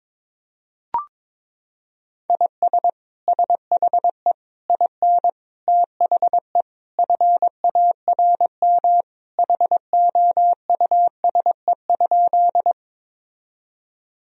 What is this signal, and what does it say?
Morse code, heard as a keyed tone, about 22 wpm, 700 Hz. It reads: IS SHE IN THE FARM HOUSE?
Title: Sentences from Top 1000 Words - Review Part 2 - 22wpm